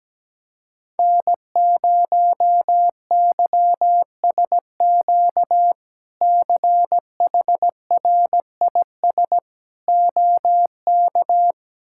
Morse code, heard as a keyed tone, about 17 wpm, 700 Hz. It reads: N0YSQ CHRIS OK
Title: K1USN SST QSO Practice - 17wpm